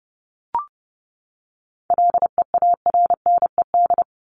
Morse code, heard as a keyed tone, about 30 wpm, 700 Hz. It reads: LEARNED
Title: ED Suffix - Top 100 Words with Base Words 30wpm